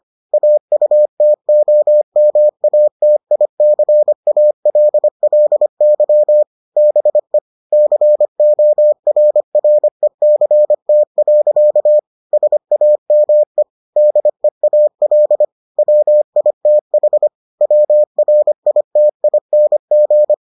Morse code, heard as a keyed tone, about 25 wpm, 600 Hz. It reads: AUTOMATICALLY BE CORRECT. SAME DEAL WITH WRITING